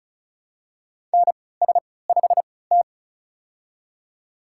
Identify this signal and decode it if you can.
Morse code, heard as a keyed tone, about 36 wpm, spaced as for 20 wpm, 700 Hz. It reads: NS5T
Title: USA Call Signs - 4 Characters - 3x Character Spacing - 35wpm